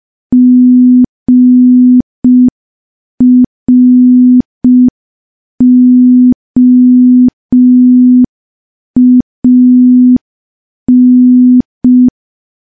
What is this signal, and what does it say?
Morse code, heard as a keyed tone, about 5 wpm, 250 Hz. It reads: GROAN
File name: morse / morse-training-05WPM